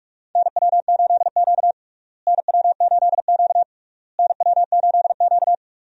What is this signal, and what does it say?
Morse code, heard as a keyed tone, about 42 wpm, 700 Hz. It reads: NW8Q NW8Q NW8Q
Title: USA Call Signs - 4 Characters - Repeated 3x - 45wpm